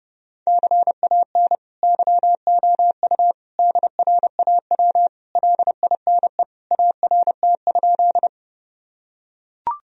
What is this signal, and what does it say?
Morse code, heard as a keyed tone, about 30 wpm, 700 Hz. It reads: CAN YOU DRAW LINE ART?
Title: Sentences from Top 1000 Words - Review Part 3 - 30wpm